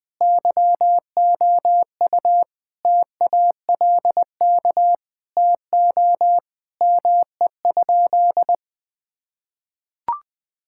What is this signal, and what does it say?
Morse code, heard as a keyed tone, about 20 wpm, 700 Hz. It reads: YOU TALK TO ME?
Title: Sentences with 2-Word Phrases - Spoken Beginning - 20wpm